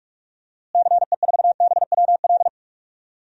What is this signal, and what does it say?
Morse code, heard as a keyed tone, about 46 wpm, 700 Hz. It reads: CE4BWL
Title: DXCC-100 Call Signs - 6 Characters - Repeated 2x - 45wpm